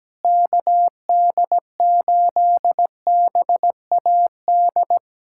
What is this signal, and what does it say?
Morse code, heard as a keyed tone, about 17 wpm, 700 Hz. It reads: KD8BAD
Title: USA Callsign DE Callsign - 6 Characters - 17wpm